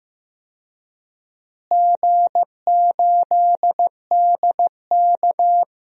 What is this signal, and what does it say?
Morse code, heard as a keyed tone, about 15 wpm, 700 Hz. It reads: G8DK
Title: DXCC-100 Call Signs - 4 Characters - Repeated 3x - 15wpm